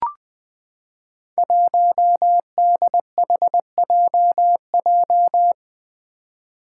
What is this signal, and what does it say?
Morse code, heard as a keyed tone, about 20 wpm, 700 Hz. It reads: 1DHJJ
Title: Five Challenging Letter-Number Combinations 20wpm